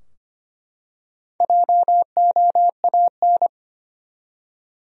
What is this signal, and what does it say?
Morse code, heard as a keyed tone, about 25 wpm, 700 Hz. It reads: JOAN
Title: USA Common Names - Max 5 Letters - 25wpm